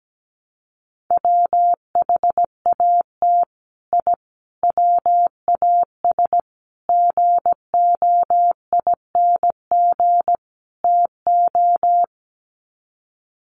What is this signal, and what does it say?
Morse code, heard as a keyed tone, about 17 wpm, 700 Hz. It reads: WHAT I WAS GOING TO
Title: Sets of 5 Words Encore2 17wpm